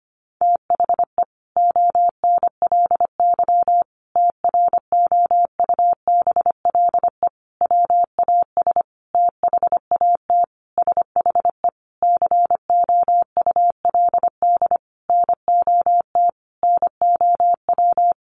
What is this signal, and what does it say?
Morse code, heard as a keyed tone, about 25 wpm, 700 Hz. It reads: THE ONLY TROUBLE WAS THAT SHE COULD NOT NOW
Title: Sentences from Top 1200 Words 25wpm